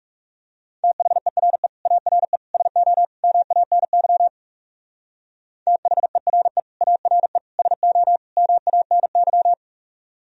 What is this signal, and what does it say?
Morse code, heard as a keyed tone, about 38 wpm, 700 Hz. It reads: THERE ARE SO MANY THERE ARE SO MANY